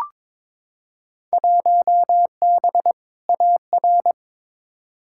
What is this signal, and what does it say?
Morse code, heard as a keyed tone, about 22 wpm, 700 Hz. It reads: 1B AR
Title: Field Day Exchanges Encore - 22wpm